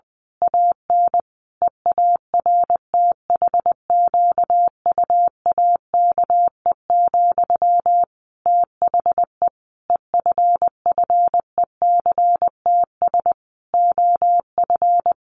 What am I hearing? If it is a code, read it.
Morse code, heard as a keyed tone, about 20 wpm, 700 Hz. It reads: AN EARTHQUAKE, THE EFFECTS OF
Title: A Sound Of Thunder 20wpm